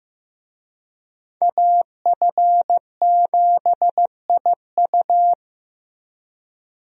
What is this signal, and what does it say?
Morse code, heard as a keyed tone, about 15 wpm, 700 Hz. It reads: AF7IU
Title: USA Call Signs - 5 Characters - Repeated 2x - 15wpm